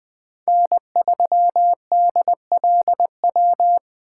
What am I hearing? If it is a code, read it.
Morse code, heard as a keyed tone, about 19 wpm, 700 Hz. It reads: N3DLW